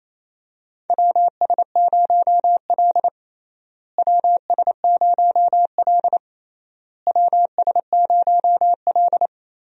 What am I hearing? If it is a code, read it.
Morse code, heard as a keyed tone, about 28 wpm, 700 Hz. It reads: WS0L WS0L WS0L
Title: USA Call Signs - 4 Characters - Repeated 3x - 28wpm